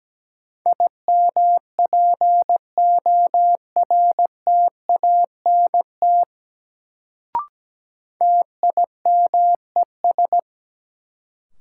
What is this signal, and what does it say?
Morse code, heard as a keyed tone, about 17 wpm, 700 Hz. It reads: IMPORTANT TIMES
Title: Sets of 3 Words - New Word Drill - 17wpm